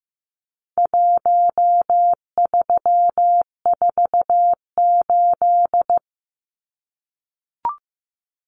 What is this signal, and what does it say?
Morse code, heard as a keyed tone, about 14 wpm, 700 Hz. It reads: 1348
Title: Numbers - Contest Style - 4-Digit 15wpm